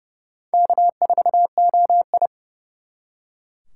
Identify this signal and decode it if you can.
Morse code, heard as a keyed tone, about 36 wpm, 700 Hz. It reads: K4OI
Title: Speed Racing - USA 1x2 Call Signs 30wpm